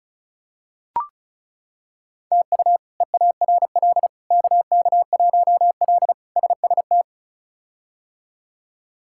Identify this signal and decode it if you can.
Morse code, heard as a keyed tone, about 36 wpm, 700 Hz. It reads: TU EARL KK1L SST
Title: K1USN SST QSO Practice - 35wpm